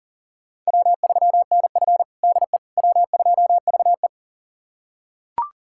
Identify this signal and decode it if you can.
Morse code, heard as a keyed tone, about 37 wpm, 700 Hz. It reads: W3NF DE W2VE
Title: USA Callsign DE Callsign - 4 Characters - 40wpm